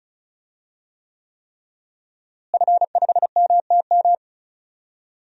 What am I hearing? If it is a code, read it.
Morse code, heard as a keyed tone, about 34 wpm, 700 Hz. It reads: F5MTM
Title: DXCC-100 Call Signs - 5 Characters - Repeated 3x - 35wpm